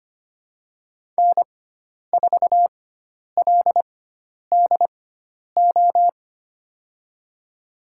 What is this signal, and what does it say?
Morse code, heard as a keyed tone, about 25 wpm, spaced as for 10 wpm, 700 Hz. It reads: N4LDO
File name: Lesson 16g - Letter c - Call Signs 25wpm